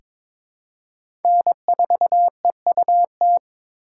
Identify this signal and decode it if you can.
Morse code, heard as a keyed tone, about 22 wpm, 700 Hz. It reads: N4EUT
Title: USA Call Signs - 5 Characters - 6x Character Spacing - 22wpm